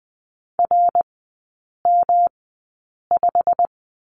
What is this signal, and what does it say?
Morse code, heard as a keyed tone, about 20 wpm, spaced as for 8 wpm, 700 Hz. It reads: RM5
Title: 205 Lesson 33e - Letter j - Three Characters 20wpm